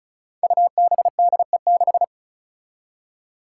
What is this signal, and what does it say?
Morse code, heard as a keyed tone, about 36 wpm, 700 Hz. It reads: UBDE6
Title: Five Challenging Letter-Number Combinations 35wpm